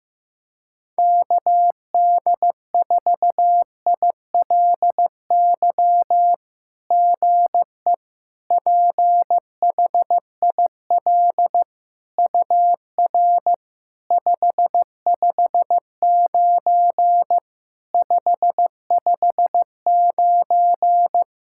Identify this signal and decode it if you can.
Morse code, heard as a keyed tone, about 15 wpm, 700 Hz. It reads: KD4ILY GE PHIL UR 559 559
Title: SOTA QSO Practice 15wpm